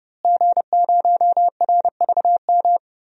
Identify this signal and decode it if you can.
Morse code, heard as a keyed tone, about 30 wpm, 700 Hz. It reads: G0RVM
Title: DXCC-100 Call Signs - 5 Characters - Repeated 3x - 30wpm